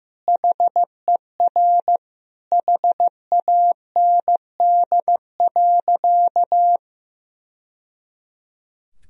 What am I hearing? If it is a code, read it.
Morse code, heard as a keyed tone, about 15 wpm, 700 Hz. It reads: HER HAND.
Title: Sentences from Top 1000 Words - Review Part 2 - 15wpm